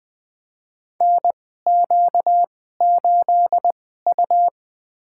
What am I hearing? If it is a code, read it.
Morse code, heard as a keyed tone, about 20 wpm, spaced as for 14 wpm, 700 Hz. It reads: NQ8U
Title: USA Call Signs - 4 Characters - 2x Character Spacing - 20wpm